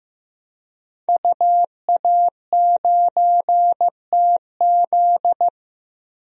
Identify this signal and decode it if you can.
Morse code, heard as a keyed tone, about 15 wpm, 700 Hz. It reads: UA9TZ
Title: DXCC-100 Call Signs - 5 Characters - 15wpm